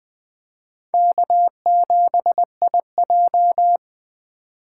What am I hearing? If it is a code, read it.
Morse code, heard as a keyed tone, about 20 wpm, 700 Hz. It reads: K7IJ
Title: Speed Racing - USA 1x2 Call Signs 17wpm